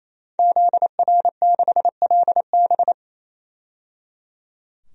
Morse code, heard as a keyed tone, about 28 wpm, 700 Hz. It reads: ZR6LB
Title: DXCC-100 Call Signs - 5 Characters - 28wpm